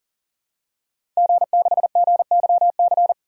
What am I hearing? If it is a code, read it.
Morse code, heard as a keyed tone, about 40 wpm, 700 Hz. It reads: G6GYC
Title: DXCC-100 Call Signs - 5 Characters - Repeated 2x - 40wpm